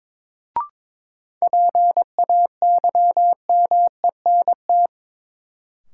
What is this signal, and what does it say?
Morse code, heard as a keyed tone, about 23 wpm, 700 Hz. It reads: PAYMENT
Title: Sentences from Top 1200 Words - New Word Drill Alt - 22wpm